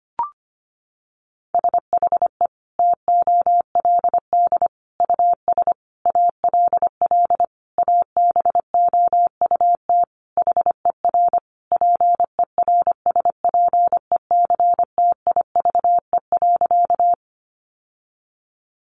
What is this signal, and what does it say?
Morse code, heard as a keyed tone, about 25 wpm, 700 Hz. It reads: SHE TOLD US ALL ABOUT HER PERSPECTIVE.
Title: Sentences from Top 1200 Words 25wpm